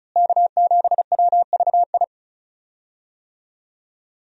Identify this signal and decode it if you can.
Morse code, heard as a keyed tone, about 34 wpm, 700 Hz. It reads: K7WVI
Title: USA Call Signs - 5 Characters - Repeated 2x - 35wpm